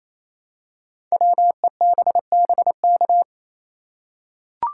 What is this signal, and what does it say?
Morse code, heard as a keyed tone, about 29 wpm, 700 Hz. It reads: WEBBK
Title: Code Groups - 5 Letter:Numbers - 28wpm